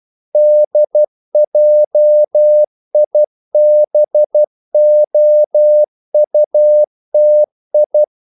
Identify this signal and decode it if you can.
Morse code, heard as a keyed tone, about 12 wpm, 600 Hz. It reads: DJIBOUTI
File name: CW / Practice / NationalCapitals12